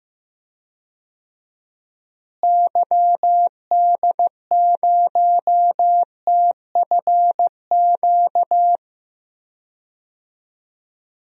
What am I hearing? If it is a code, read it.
Morse code, heard as a keyed tone, about 15 wpm, 700 Hz. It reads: YD0TFQ
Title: DXCC-100 Call Signs - 6 Characters - Repeated 3x - 15wpm